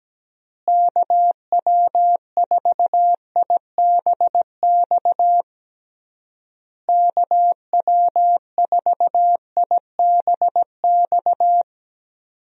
Morse code, heard as a keyed tone, about 17 wpm, 700 Hz. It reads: KW4IBX KW4IBX
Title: USA Call Signs - 6 Characters - Repeated 3x - 17wpm